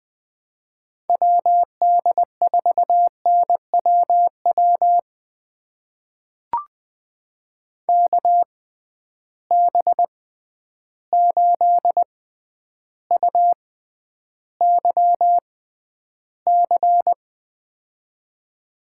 Morse code, heard as a keyed tone, about 21 wpm, spaced as for 10 wpm, 700 Hz. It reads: WD4NWW KB8UYC
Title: USA Call Signs - 6 Characters - 6x Character Spacing - 20wpm